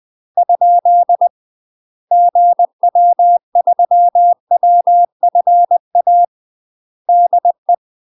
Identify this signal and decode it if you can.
Morse code, heard as a keyed tone, about 20 wpm, 700 Hz. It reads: ? GW3WFA DE
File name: UK MASTER QSOs TRAINING / 20WPM / UK MASTER QSOs 0026